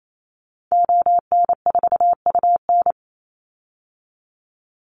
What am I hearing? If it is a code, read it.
Morse code, heard as a keyed tone, about 28 wpm, 700 Hz. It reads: ON4UN
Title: Call Signs Repeated 3x - 28wpm